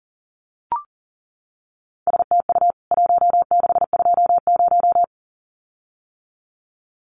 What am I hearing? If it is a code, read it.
Morse code, heard as a keyed tone, about 38 wpm, 700 Hz. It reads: STU 1620
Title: CWT Exchange 40wpm